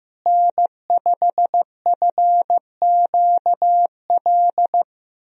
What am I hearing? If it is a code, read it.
Morse code, heard as a keyed tone, about 15 wpm, 700 Hz. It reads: N5FQL
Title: USA Call Signs - 5 Characters - Repeated 2x - 15wpm